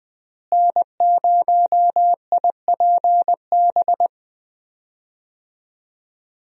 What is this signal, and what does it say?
Morse code, heard as a keyed tone, about 20 wpm, 700 Hz. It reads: N0IPB
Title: USA Call Signs - 5 Characters - Repeated 2x - 20wpm